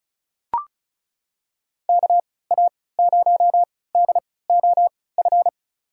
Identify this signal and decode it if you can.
Morse code, heard as a keyed tone, about 34 wpm, spaced as for 20 wpm, 700 Hz. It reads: KA0DOF
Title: USA Call Signs - 3x Character Spacing - 35wpm